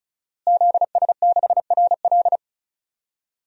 Morse code, heard as a keyed tone, about 34 wpm, 700 Hz. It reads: ZS6RL
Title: DXCC-100 Call Signs - 5 Characters - Repeated 3x - 35wpm